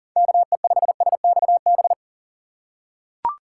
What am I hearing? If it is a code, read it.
Morse code, heard as a keyed tone, about 40 wpm, 700 Hz. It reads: KE5SXB
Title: USA Call Signs - 6 Characters - 4x Character Spacing - 40wpm